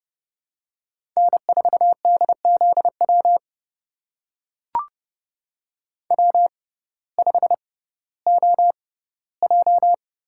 Morse code, heard as a keyed tone, about 32 wpm, spaced as for 17 wpm, 700 Hz. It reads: N4DZW W5OJ
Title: USA Call Signs - 5 Characters - 6x Character Spacing - 30wpm